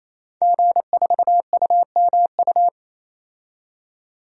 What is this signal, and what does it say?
Morse code, heard as a keyed tone, about 28 wpm, 700 Hz. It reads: G4UMU
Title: DXCC-100 Call Signs - 5 Characters - 28wpm